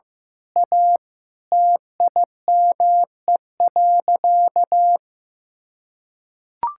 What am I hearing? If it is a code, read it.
Morse code, heard as a keyed tone, about 16 wpm, 700 Hz. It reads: A TIME.
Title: Sentences from Top 1000 Words - Review Part 6 - 15wpm